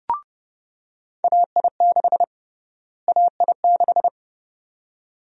Morse code, heard as a keyed tone, about 30 wpm, 700 Hz. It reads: AI6 AI6
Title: USA Call Sign Prefixes - Repeated 2x - 30wpm